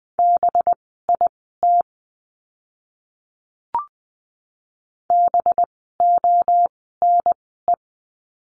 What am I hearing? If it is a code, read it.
Morse code, heard as a keyed tone, about 20 wpm, spaced as for 14 wpm, 700 Hz. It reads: BIT BONE
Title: 185 Lesson 30f - Letter b - Words 20wpm